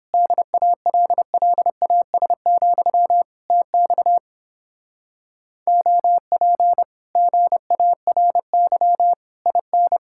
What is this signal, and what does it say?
Morse code, heard as a keyed tone, about 29 wpm, 700 Hz. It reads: DALLAS, TX OP GARY IN